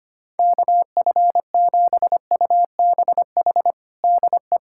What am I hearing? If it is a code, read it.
Morse code, heard as a keyed tone, about 25 wpm, 700 Hz. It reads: KF7UBH DE
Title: USA Callsign DE Callsign - 6 Characters - 25wpm